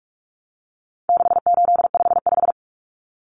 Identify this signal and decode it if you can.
Morse code, heard as a keyed tone, about 44 wpm, 700 Hz. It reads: 6855
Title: Common Cut Numbers - 4-Digit 45wpm